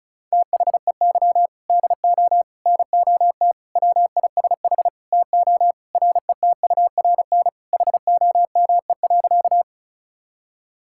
Morse code, heard as a keyed tone, about 36 wpm, 700 Hz. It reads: THEY DO NOT WISH TO RETURN HOME.
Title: Sentences from Top 1600 Words 35wpm